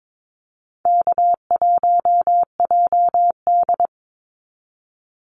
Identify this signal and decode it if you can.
Morse code, heard as a keyed tone, about 22 wpm, 700 Hz. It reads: K1JD
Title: NA SOTA Activator Callsigns 22wpm